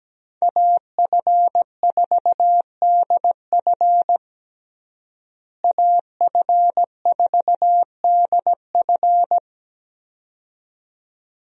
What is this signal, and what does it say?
Morse code, heard as a keyed tone, about 17 wpm, 700 Hz. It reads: AF4DF AF4DF
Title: USA Call Signs - 5 Characters - Repeated 2x - 17wpm